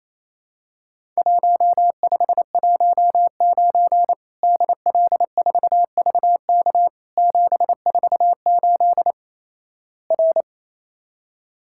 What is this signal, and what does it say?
Morse code, heard as a keyed tone, about 28 wpm, 700 Hz. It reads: 1519 DL4VK 748 R
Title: Worked All Europe DX Contest - QTC Exchange - 28wpm